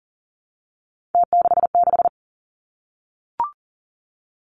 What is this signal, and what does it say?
Morse code, heard as a keyed tone, about 38 wpm, 690 Hz. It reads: T66
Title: Three Challenging Letter-Number Combinations 40wpm